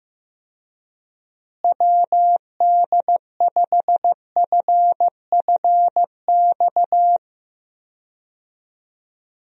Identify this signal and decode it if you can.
Morse code, heard as a keyed tone, about 15 wpm, 700 Hz. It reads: WD5FFX
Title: USA Call Signs - 6 Characters - Repeated 3x - 15wpm